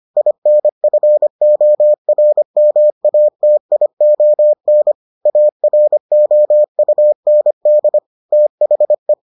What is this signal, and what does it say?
Morse code, heard as a keyed tone, about 25 wpm, 600 Hz. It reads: INFORMATION AROUND THE